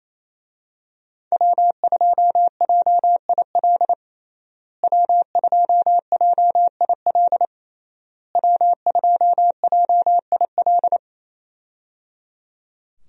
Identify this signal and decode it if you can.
Morse code, heard as a keyed tone, about 28 wpm, 700 Hz. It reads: W2JIL W2JIL W2JIL
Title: USA Call Signs - 5 Characters - Repeated 3x - 28wpm